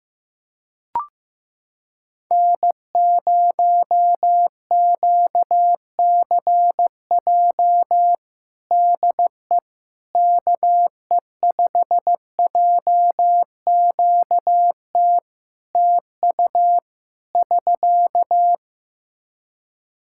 Morse code, heard as a keyed tone, about 15 wpm, 700 Hz. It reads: N0QCJ DE KE5JQT TU <SK>
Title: Practice QSO - Exchange 4 - Ending - 15wpm